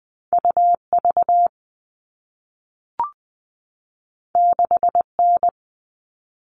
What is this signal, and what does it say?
Morse code, heard as a keyed tone, about 20 wpm, 700 Hz. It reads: UV 6N
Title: All Cut Numbers - 2-Digit 20wpm